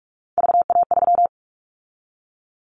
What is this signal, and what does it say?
Morse code, heard as a keyed tone, about 46 wpm, 700 Hz. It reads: VA3
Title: Callsign Prefixes for Top 25 DXCC - 45wpm